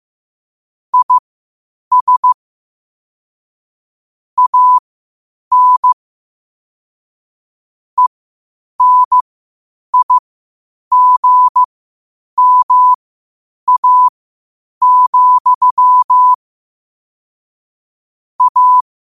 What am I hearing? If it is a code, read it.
Morse code, heard as a keyed tone, about 15 wpm, spaced as for 8 wpm, 1000 Hz. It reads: IS AN ENIGMA, A
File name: qotdfiles / 8wpm